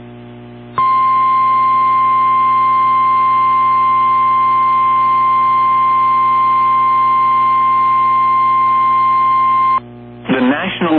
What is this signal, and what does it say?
Emergency Alert System (SAME) tones (the 1050 Hz attention tone), with speech. It is heard from 0.7 to 9.8 s.
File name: eas / KWBE / KWBE